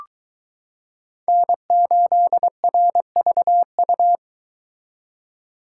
Morse code, heard as a keyed tone, about 22 wpm, 700 Hz. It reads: N8RVU